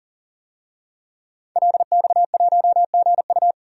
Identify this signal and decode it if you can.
Morse code, heard as a keyed tone, about 40 wpm, 700 Hz. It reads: LX1GU
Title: DXCC-100 Call Signs - 5 Characters - 40wpm